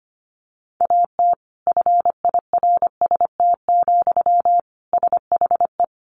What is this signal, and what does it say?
Morse code, heard as a keyed tone, about 25 wpm, 700 Hz. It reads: AT FIRST, SHE
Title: Sentences from Top 1400 Words 25wpm